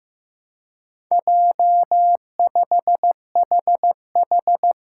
Morse code, heard as a keyed tone, about 15 wpm, 700 Hz. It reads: J5HH
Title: Four Challenging Letter-Number Combinations 15wpm